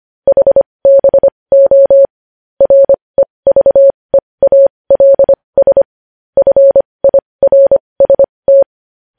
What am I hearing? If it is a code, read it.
Morse code, heard as a keyed tone, about 25 wpm, spaced as for 20 wpm, 550 Hz. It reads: HBO REVEALS FIRST
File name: podcast / News headlines-1x-25x20